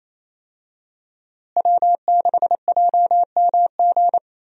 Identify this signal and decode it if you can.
Morse code, heard as a keyed tone, about 28 wpm, 700 Hz. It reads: W6JMG
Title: USA Call Signs - 5 Characters - Repeated 2x - 28wpm